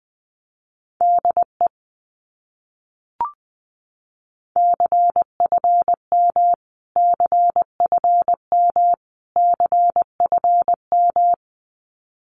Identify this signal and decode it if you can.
Morse code, heard as a keyed tone, about 20 wpm, 700 Hz. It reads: DE CFM CFM CFM
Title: Abbreviations Repeated 3x - 20wpm